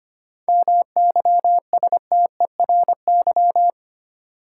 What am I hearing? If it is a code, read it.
Morse code, heard as a keyed tone, about 25 wpm, 700 Hz. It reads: MYSTERY
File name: practice / 25wpm / Difficult Words with Hints - 25wpm